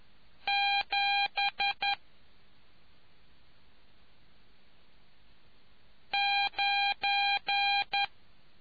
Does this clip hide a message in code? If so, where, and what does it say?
Yes: Morse code from 0.5 to 8.0 s, heard as a keyed tone, about 11 wpm, 800 Hz. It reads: 7 9